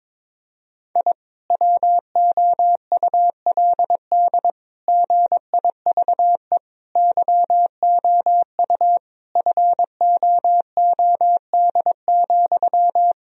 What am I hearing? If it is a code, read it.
Morse code, heard as a keyed tone, about 22 wpm, 700 Hz. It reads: I WOULD GIVE YOU FOOD,
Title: Sentences from Top 1000 Words - Review Part 1 - 22wpm